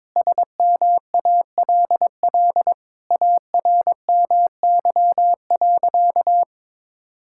Morse code, heard as a keyed tone, about 21 wpm, 700 Hz. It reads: SMALL ARMY.
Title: Sentences from Top 1000 Words - Review Part 6 - 22wpm